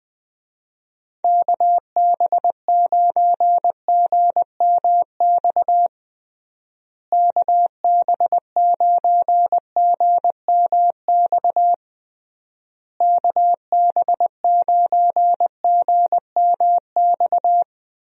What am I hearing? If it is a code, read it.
Morse code, heard as a keyed tone, about 20 wpm, 700 Hz. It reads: KB9GMX KB9GMX KB9GMX
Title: USA Call Signs - 6 Characters - Repeated 3x - 20wpm